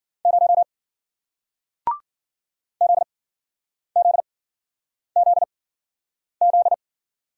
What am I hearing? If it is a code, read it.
Morse code, heard as a keyed tone, about 56 wpm, spaced as for 9 wpm, 700 Hz. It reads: 0 ZZZZ